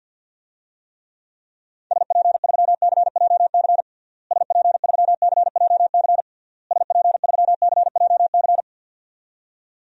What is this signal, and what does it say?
Morse code, heard as a keyed tone, about 47 wpm, 700 Hz. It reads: SP3CJC SP3CJC SP3CJC
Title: DXCC-100 Call Signs - 6 Characters - Repeated 3x - 50wpm